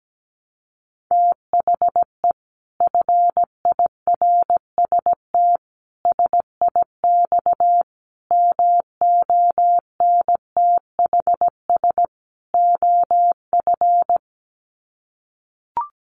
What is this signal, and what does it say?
Morse code, heard as a keyed tone, about 17 wpm, 700 Hz. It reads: THE FIRST SIX MONTHS OF
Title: Sets of 5 Words Encore2 17wpm